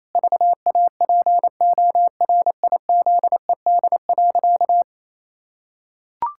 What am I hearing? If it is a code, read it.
Morse code, heard as a keyed tone, about 28 wpm, 700 Hz. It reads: VAPORIZED.
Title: Pangram Challenge v2 - 28wpm